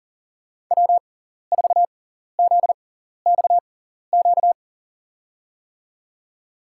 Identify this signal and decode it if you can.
Morse code, heard as a keyed tone, about 40 wpm, spaced as for 14 wpm, 700 Hz. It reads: W4ZXQ
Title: USA Call Signs - 6x Character Spacing - 40wpm